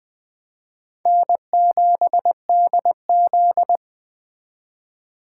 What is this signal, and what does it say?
Morse code, heard as a keyed tone, about 20 wpm, 700 Hz. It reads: N7DZ
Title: Speed Racing - USA 1x2 Call Signs 17wpm